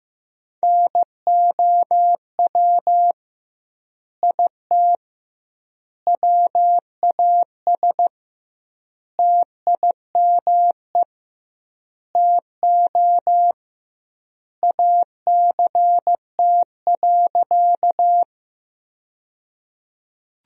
Morse code, heard as a keyed tone, about 15 wpm, 700 Hz. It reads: NOW IT WAS TIME TO ACT.
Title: 2x Word-Spacing - Sentences from Top 200 Words - 15wpm